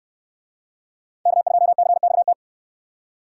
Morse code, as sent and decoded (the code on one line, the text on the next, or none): -.- ..--- -.-. -..- -
K2CXT